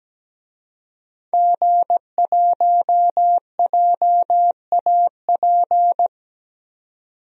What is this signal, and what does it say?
Morse code, heard as a keyed tone, about 17 wpm, 700 Hz. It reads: G1JAP
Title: DXCC-100 Call Signs - 5 Characters - 17wpm